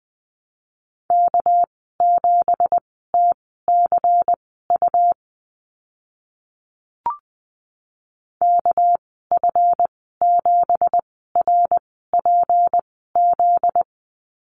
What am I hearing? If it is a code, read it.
Morse code, heard as a keyed tone, about 20 wpm, spaced as for 14 wpm, 700 Hz. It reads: K7TCU KF7RPZ
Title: 245 Lesson 39g - Letter z - Call Signs 20wpm